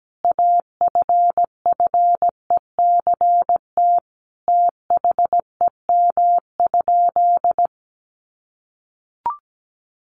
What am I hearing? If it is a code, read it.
Morse code, heard as a keyed tone, about 17 wpm, 700 Hz. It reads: AFFECT THEM?
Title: Sentences from Top 900 Words 17wpm